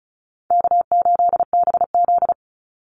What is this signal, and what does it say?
Morse code, heard as a keyed tone, about 34 wpm, 700 Hz. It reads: K8BZ
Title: Easy Call Signs Repeated 2x - 35wpm